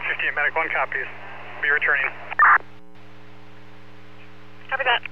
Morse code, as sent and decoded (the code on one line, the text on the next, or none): none